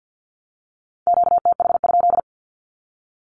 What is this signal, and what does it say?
Morse code, heard as a keyed tone, about 48 wpm, 700 Hz. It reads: QTH?
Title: Q-Codes Repeated 3x 50wpm